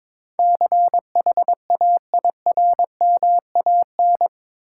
Morse code, heard as a keyed tone, about 22 wpm, 700 Hz. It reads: CHAIRMAN
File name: practice / 22wpm / Eight Letter Words - 22wpm